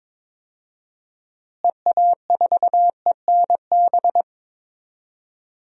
Morse code, heard as a keyed tone, about 22 wpm, 700 Hz. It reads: EA4ENB
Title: DXCC-100 Call Signs - 6 Characters - Repeated 3x - 22wpm